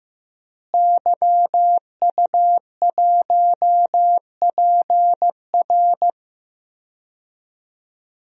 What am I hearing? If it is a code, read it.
Morse code, heard as a keyed tone, about 15 wpm, 700 Hz. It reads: YU1PR
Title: DXCC-100 Call Signs - 5 Characters - Repeated 2x - 15wpm